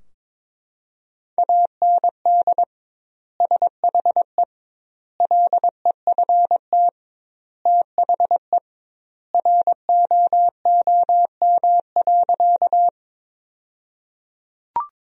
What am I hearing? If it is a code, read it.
Morse code, heard as a keyed tone, about 22 wpm, 700 Hz. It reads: AND SHE LEFT THE ROOM.
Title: 2x Word-Spacing - Sentences from Top 300 Words - 22wpm